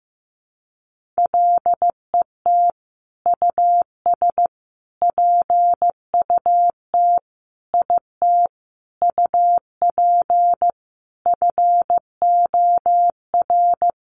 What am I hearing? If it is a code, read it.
Morse code, heard as a keyed tone, about 15 wpm, 700 Hz. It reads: LET US PUT IT UP FOR
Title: Sentences from Top 1400 Words 15wpm